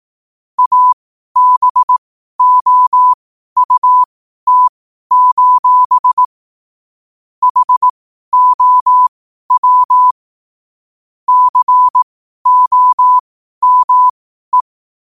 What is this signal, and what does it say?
Morse code, heard as a keyed tone, about 18 wpm, spaced as for 12 wpm, 1000 Hz. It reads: ABOUT: HOW COME